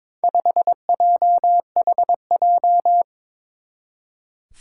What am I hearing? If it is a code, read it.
Morse code, heard as a keyed tone, about 21 wpm, 700 Hz. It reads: 5JHJ
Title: Four Challenging Letter-Number Combinations 22wpm